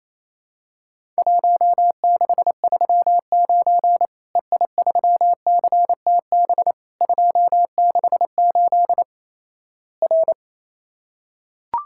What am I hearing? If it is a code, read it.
Morse code, heard as a keyed tone, about 27 wpm, 700 Hz. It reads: 1639 EI3CTB 268 R
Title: Worked All Europe DX Contest - QTC Exchange - 28wpm